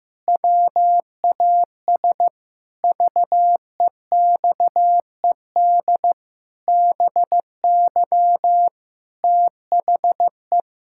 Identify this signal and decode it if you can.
Morse code, heard as a keyed tone, about 15 wpm, 700 Hz. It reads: WAS VEXED BY THE
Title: Pangram Challenge v2 - 15wpm